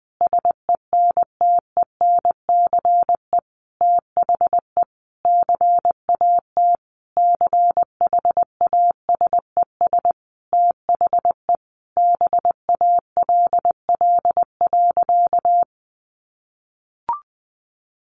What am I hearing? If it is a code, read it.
Morse code, heard as a keyed tone, about 20 wpm, 700 Hz. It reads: SENTENCE THE CAT CHASES THE BALL.
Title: Additional Easy Phrases 20wpm